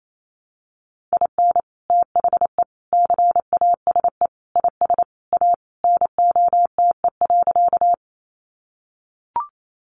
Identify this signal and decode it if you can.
Morse code, heard as a keyed tone, about 28 wpm, 700 Hz. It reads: IN THE CASE IS A NOTE.
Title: Sentences from Top 700 Words 28wpm